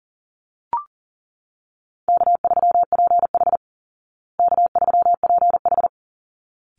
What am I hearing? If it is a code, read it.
Morse code, heard as a keyed tone, about 40 wpm, 700 Hz. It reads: K3PH K3PH
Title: Easy Call Signs Repeated 2x - 40wpm